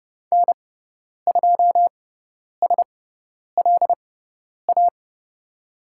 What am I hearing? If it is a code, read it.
Morse code, heard as a keyed tone, about 30 wpm, spaced as for 10 wpm, 700 Hz. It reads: N2SLA
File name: Lesson 16g - Letter c - Call Signs 30wpm